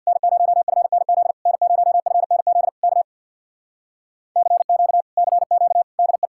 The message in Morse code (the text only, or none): N0FNZ N0FNZ K CQ CQ DE